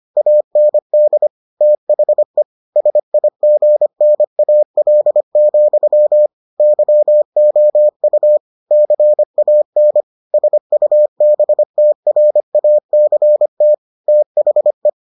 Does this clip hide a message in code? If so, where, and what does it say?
Yes: Morse code from 0.2 to 14.9 s, heard as a keyed tone, about 25 wpm, 600 Hz. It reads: AND THE SIGNAL, YOU CAN SUBTRACT THE